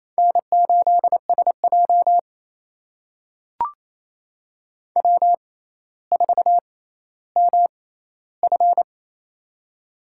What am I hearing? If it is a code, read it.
Morse code, heard as a keyed tone, about 28 wpm, spaced as for 14 wpm, 700 Hz. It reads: N8SJ W4MF
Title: USA Call Signs - 4 Characters - 6x Character Spacing - 28wpm